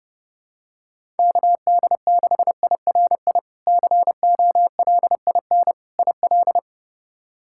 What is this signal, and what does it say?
Morse code, heard as a keyed tone, about 30 wpm, 700 Hz. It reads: KD6IRI COLIN IL
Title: K1USN SST QSO Practice - 30wpm